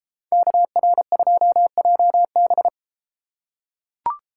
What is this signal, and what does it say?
Morse code, heard as a keyed tone, about 31 wpm, 700 Hz. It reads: KR2JB